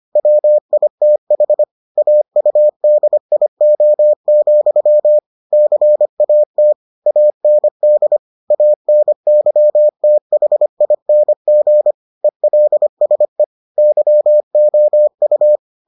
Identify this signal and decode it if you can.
Morse code, heard as a keyed tone, about 25 wpm, 600 Hz. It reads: WITH AUDIO, CAT AND ANYTHING ELSE YOU